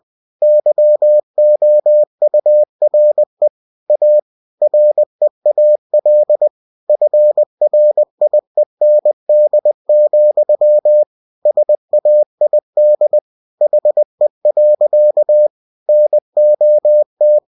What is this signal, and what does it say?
Morse code, heard as a keyed tone, about 20 wpm, 600 Hz. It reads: YOURE A REAL FRIEND, SAID HE. NOT